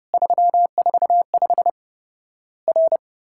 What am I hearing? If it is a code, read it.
Morse code, heard as a keyed tone, about 30 wpm, 700 Hz. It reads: 345 R
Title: Worked All Europe DX Contest - QTC Exchange - 30wpm